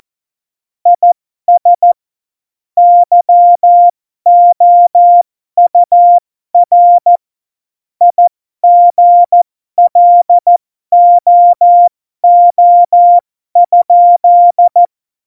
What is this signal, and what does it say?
Morse code, heard as a keyed tone, about 14 wpm, spaced as for 12 wpm, 700 Hz. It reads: IS YOUR IGLOO?
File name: Morsekurs / Witze